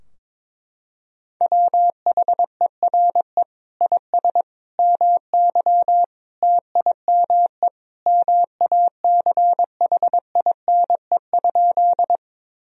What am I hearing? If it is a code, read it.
Morse code, heard as a keyed tone, about 22 wpm, 700 Hz. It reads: WHERE IS MY TIME MACHINE?
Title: Sentences from Top 1000 Words - Review Part 2 - 22wpm